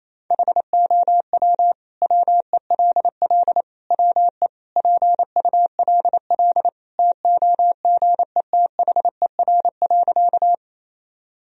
Morse code, heard as a keyed tone, about 27 wpm, 700 Hz. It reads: HOW WELL WE PULL TOGETHER.
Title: Sentences from Top 1000 Words - Review Part 2 - 28wpm